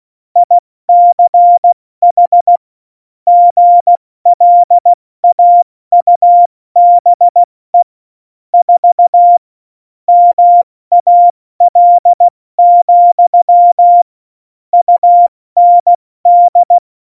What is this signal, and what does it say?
Morse code, heard as a keyed tone, about 16 wpm, 700 Hz. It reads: ICH GLAUBE 4 MAL, UND
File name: Morsekurs / Witze / Witz 18 August 2020, 70 F80